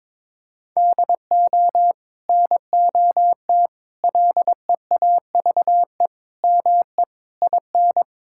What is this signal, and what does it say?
Morse code, heard as a keyed tone, about 21 wpm, 700 Hz. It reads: DO NOT LEAVE ME IN
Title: Sentences from Top 1600 Words 22wpm